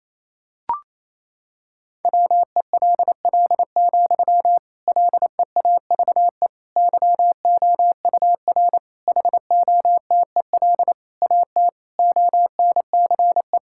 Morse code, heard as a keyed tone, about 28 wpm, 700 Hz. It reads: WELL, LEAVE YOUR HOTEL AT ONCE
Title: Sentences from Top 1600 Words 28wpm